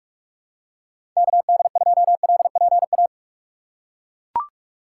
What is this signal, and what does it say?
Morse code, heard as a keyed tone, about 42 wpm, 700 Hz. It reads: KD2LPA
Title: USA Call Signs - 6 Characters - 6x Character Spacing - 45wpm